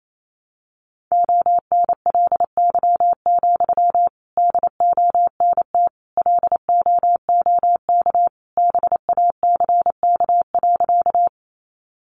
Morse code, heard as a keyed tone, about 28 wpm, 700 Hz. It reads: ONLY, DONT LOOK BACK.
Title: Sentences from Top 300 Words 28wpm